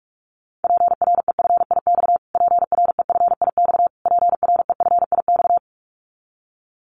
Morse code, heard as a keyed tone, about 44 wpm, 700 Hz. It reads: PREFIX PREFIX PREFIX
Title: Callsign Prefixes for Top 25 DXCC - Repeated 3x - 45wpm